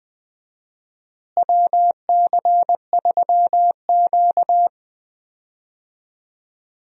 Morse code, heard as a keyed tone, about 20 wpm, 700 Hz. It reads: WC3Q
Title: USA Call Signs - 4 Characters - Repeated 2x - 20wpm